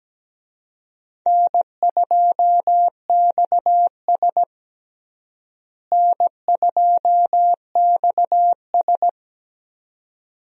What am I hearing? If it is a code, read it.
Morse code, heard as a keyed tone, about 17 wpm, 700 Hz. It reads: N2XS N2XS
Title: USA Call Signs - 4 Characters - Repeated 2x - 17wpm